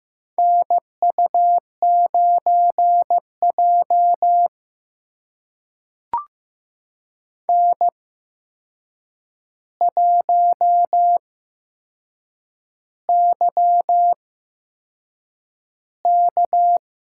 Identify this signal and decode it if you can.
Morse code, heard as a keyed tone, about 15 wpm, spaced as for 6 wpm, 700 Hz. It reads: NU9J N1YK
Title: USA Call Signs - 4 Characters - 8x Character Spacing - 15wpm